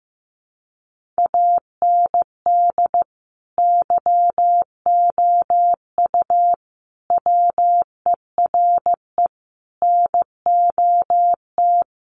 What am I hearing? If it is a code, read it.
Morse code, heard as a keyed tone, about 15 wpm, 700 Hz. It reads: AND YOU WERE NOT
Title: Sentences from Top 1400 Words 15wpm